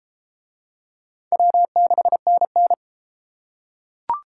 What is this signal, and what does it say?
Morse code, heard as a keyed tone, about 28 wpm, 700 Hz. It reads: W6NN